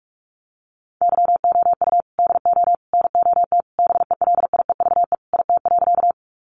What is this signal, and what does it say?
Morse code, heard as a keyed tone, about 44 wpm, 700 Hz. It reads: YOU DO NOT BELIEVE IT.
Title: Sentences from Top 1000 Words 45wpm